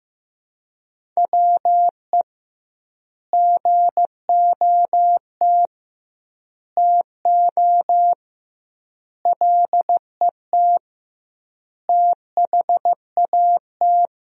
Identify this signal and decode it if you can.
Morse code, heard as a keyed tone, about 15 wpm, 700 Hz. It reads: WE GOT TO LET THAT
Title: 2x Word-Spacing - Sentences from Top 300 Words - 15wpm